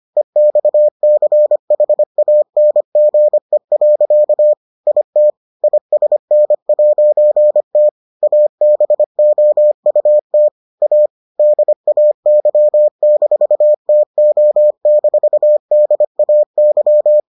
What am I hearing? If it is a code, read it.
Morse code, heard as a keyed tone, about 25 wpm, 600 Hz. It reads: EXCHANGE. IT ISN'T ABOUT A DAY-TO-DAY